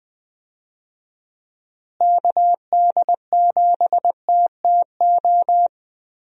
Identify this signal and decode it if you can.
Morse code, heard as a keyed tone, about 20 wpm, 700 Hz. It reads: KD7TTO
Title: USA Call Signs - 6 Characters - Repeated 3x - 20wpm